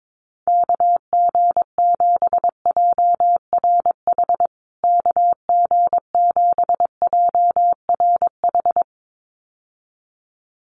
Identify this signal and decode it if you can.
Morse code, heard as a keyed tone, about 22 wpm, 700 Hz. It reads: KG7JRH KG7JRH
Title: USA Call Signs Repeated 2x 22wpm